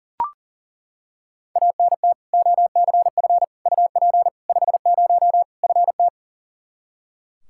Morse code, heard as a keyed tone, about 37 wpm, 700 Hz. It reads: ANT OCF UP 50 FT
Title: Practice QSO - Exchange 2 - Info - 40wpm